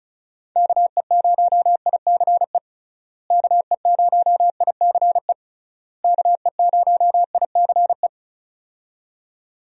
Morse code, heard as a keyed tone, about 36 wpm, 700 Hz. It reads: KE0ICE KE0ICE KE0ICE
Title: USA Call Signs - 6 Characters - Repeated 3x - 35wpm